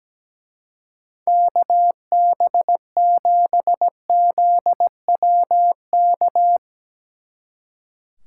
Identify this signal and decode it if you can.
Morse code, heard as a keyed tone, about 20 wpm, 700 Hz. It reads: KB7ZWK
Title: Speed Racing - USA 2x3 Call Signs 17wpm